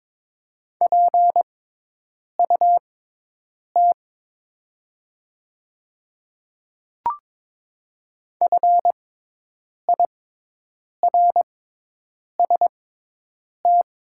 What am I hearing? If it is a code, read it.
Morse code, heard as a keyed tone, about 22 wpm, spaced as for 8 wpm, 690 Hz. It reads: PUT FIRST
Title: 6x Character Spacing - Top 500 Words - 22wpm